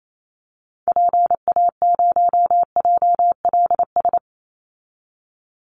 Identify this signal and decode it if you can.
Morse code, heard as a keyed tone, about 28 wpm, 700 Hz. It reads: PA0JLS
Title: Call Signs Repeated 2x - 28wpm